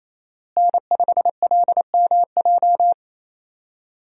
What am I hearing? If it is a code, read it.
Morse code, heard as a keyed tone, about 28 wpm, 700 Hz. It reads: N5LMJ
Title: USA Call Signs - 5 Characters - Repeated 3x - 28wpm